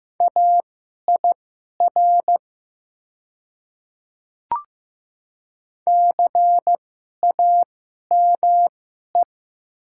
Morse code, heard as a keyed tone, about 15 wpm, spaced as for 11 wpm, 700 Hz. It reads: AIR CAME
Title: 2x Character Spacing - Top 200 Words - 15wpm